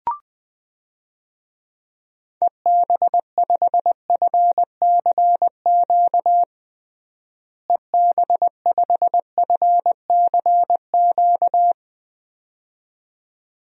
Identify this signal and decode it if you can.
Morse code, heard as a keyed tone, about 20 wpm, 700 Hz. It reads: EB5FCQ EB5FCQ
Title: DXCC-100 Call Signs - 6 Characters - Repeated 2x - 20wpm